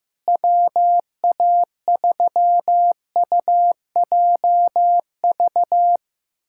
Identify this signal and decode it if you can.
Morse code, heard as a keyed tone, about 15 wpm, 700 Hz. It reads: WA3UJV
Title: USA Callsign DE Callsign 15wpm